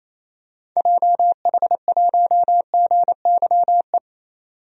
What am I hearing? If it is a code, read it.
Morse code, heard as a keyed tone, about 28 wpm, 700 Hz. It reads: JH1GYE
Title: DXCC-100 Call Signs - 6 Characters - 28wpm